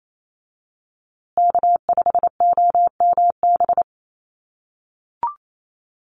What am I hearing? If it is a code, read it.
Morse code, heard as a keyed tone, about 28 wpm, 700 Hz. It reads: K5OMB
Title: USA Call Signs Repeated 2x 28wpm